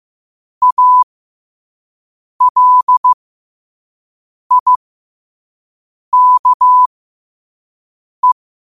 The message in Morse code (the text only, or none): ALIKE